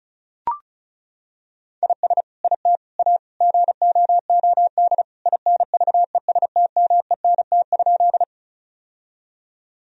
Morse code, heard as a keyed tone, about 34 wpm, 700 Hz. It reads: IS IT A GOOD INVESTMENT?
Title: Sentences from Top 1000 Words - Review Part 4 - 35wpm